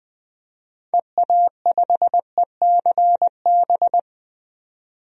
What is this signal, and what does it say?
Morse code, heard as a keyed tone, about 20 wpm, 700 Hz. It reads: EA5ECB
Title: DXCC-100 Call Signs - 6 Characters - 20wpm